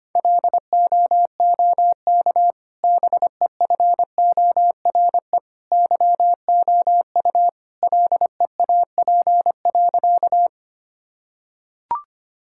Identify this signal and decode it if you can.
Morse code, heard as a keyed tone, about 25 wpm, 700 Hz. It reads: LOOK BEFORE YOU LEAP.
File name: Easy Sentences with Hints - 25wpm